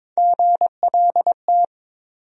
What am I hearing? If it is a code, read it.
Morse code, heard as a keyed tone, about 22 wpm, 700 Hz. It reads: GLT
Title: USA Call Sign Postfixes - 3 Letters 22wpm